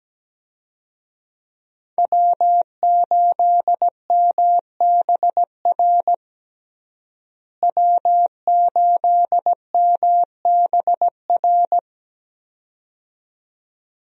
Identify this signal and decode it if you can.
Morse code, heard as a keyed tone, about 17 wpm, 700 Hz. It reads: W8MBR W8MBR
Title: USA Call Signs - 5 Characters - Repeated 2x - 17wpm